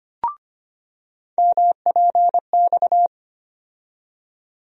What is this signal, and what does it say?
Morse code, heard as a keyed tone, about 25 wpm, 700 Hz. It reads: MPX
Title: USA Call Sign Postfixes - 3 Letters 25wpm